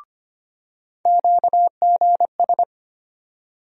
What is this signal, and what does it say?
Morse code, heard as a keyed tone, about 25 wpm, 700 Hz. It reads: QGS
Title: USA Call Sign Postfixes - 3 Letters 25wpm